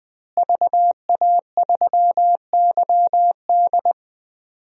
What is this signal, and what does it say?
Morse code, heard as a keyed tone, about 21 wpm, 700 Hz. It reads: VA3YD
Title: DXCC-100 Call Signs - 5 Characters - Repeated 2x - 20wpm